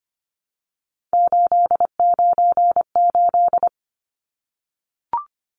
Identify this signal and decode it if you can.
Morse code, heard as a keyed tone, about 25 wpm, 700 Hz. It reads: 898
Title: Numbers - 3-Digit 25wpm